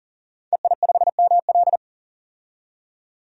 Morse code, heard as a keyed tone, about 40 wpm, 700 Hz. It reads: EI5ML
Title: DXCC-100 Call Signs - 5 Characters - 40wpm